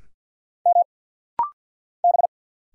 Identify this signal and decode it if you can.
Morse code, heard as a keyed tone, about 50 wpm, 700 Hz. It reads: M B
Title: Single Letter-Number Alt - Rapid-Fire 50wpm